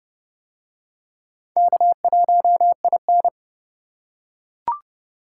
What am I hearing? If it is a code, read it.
Morse code, heard as a keyed tone, about 25 wpm, 700 Hz. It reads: K1IN
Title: Speed Racing - USA 1x2 Call Signs 20wpm